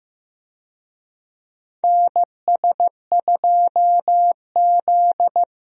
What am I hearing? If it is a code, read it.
Morse code, heard as a keyed tone, about 15 wpm, 700 Hz. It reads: NS2Z
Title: USA Call Signs - 4 Characters - Repeated 2x - 15wpm